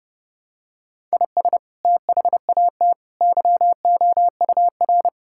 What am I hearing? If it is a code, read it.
Morse code, heard as a keyed tone, about 32 wpm, 700 Hz. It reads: IS THAT YOUR
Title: Sentences from Top 1000 Words - Review Part 5 - 30wpm